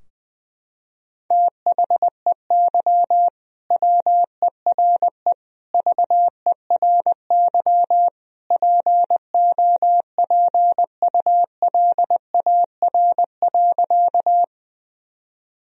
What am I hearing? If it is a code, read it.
Morse code, heard as a keyed tone, about 20 wpm, 700 Hz. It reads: THEY WERE VERY POPULAR.
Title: Sentences from Top 1000 Words - Review Part 3 - 20wpm